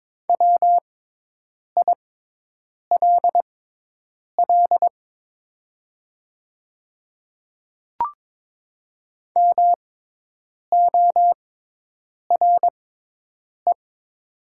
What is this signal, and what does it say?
Morse code, heard as a keyed tone, about 22 wpm, spaced as for 8 wpm, 700 Hz. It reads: WILL MORE
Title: 6x Character Spacing - Top 100 Words - 22wpm